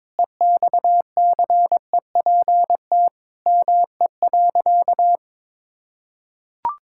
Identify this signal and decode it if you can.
Morse code, heard as a keyed tone, about 22 wpm, 700 Hz. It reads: EXCEPT ME.
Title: Sentences from Top 1600 Words 22wpm